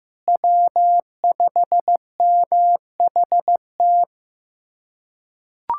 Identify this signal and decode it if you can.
Morse code, heard as a keyed tone, about 15 wpm, 700 Hz. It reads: W5MHT
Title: USA Callsign DE Callsign - 5 Characters - 15wpm